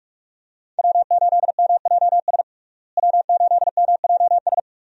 Morse code, heard as a keyed tone, about 46 wpm, 700 Hz. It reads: W8MJS W8MJS
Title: USA Call Signs - 5 Characters - Repeated 2x - 45wpm